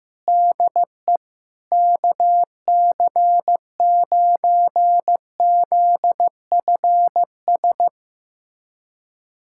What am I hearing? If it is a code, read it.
Morse code, heard as a keyed tone, about 15 wpm, 700 Hz. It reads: DE KC9ZFS
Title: USA Callsign DE Callsign - 6 Characters - 15wpm